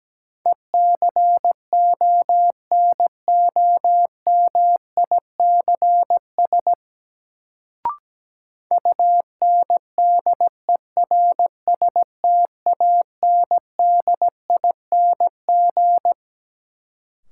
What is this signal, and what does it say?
Morse code, heard as a keyed tone, about 17 wpm, 700 Hz. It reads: ECONOMICS UNDERSTANDING
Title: Sentences from Top 1000 Words - Review Part 3 - New Word Drill - 17wpm